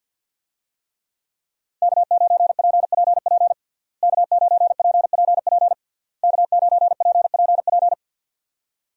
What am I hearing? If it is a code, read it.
Morse code, heard as a keyed tone, about 50 wpm, 700 Hz. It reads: K9PPP K9PPP K9PPP
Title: USA Call Signs - 5 Characters - Repeated 3x - 50wpm